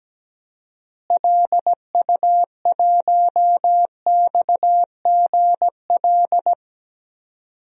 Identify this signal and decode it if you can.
Morse code, heard as a keyed tone, about 17 wpm, 700 Hz. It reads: LU1XGL
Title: DXCC-100 Call Signs - 6 Characters - Repeated 3x - 17wpm